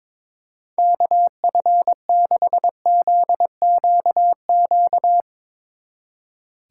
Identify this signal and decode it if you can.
Morse code, heard as a keyed tone, about 22 wpm, 700 Hz. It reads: KF6ZQQ
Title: Speed Racing - USA 2x3 Call Signs 22wpm